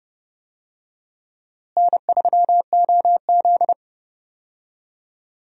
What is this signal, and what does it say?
Morse code, heard as a keyed tone, about 31 wpm, 700 Hz. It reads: N3OZ
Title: USA Call Signs - 4 Characters - Repeated 2x - 30wpm